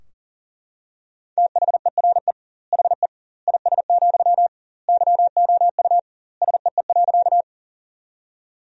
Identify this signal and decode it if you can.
Morse code, heard as a keyed tone, about 39 wpm, 700 Hz. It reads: THERE HE IS, YOU SEE.
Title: 2x Word-Spacing - Sentences from Top 100 Words - 40wpm